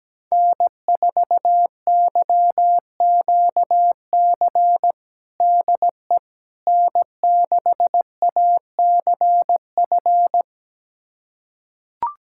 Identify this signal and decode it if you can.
Morse code, heard as a keyed tone, about 17 wpm, 700 Hz. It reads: N4YQC DE N6ACF
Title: USA Callsign DE Callsign - 5 Characters - 17wpm